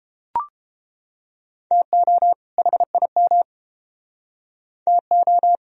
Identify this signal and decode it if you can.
Morse code, heard as a keyed tone, about 29 wpm, 700 Hz. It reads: TO HIM TO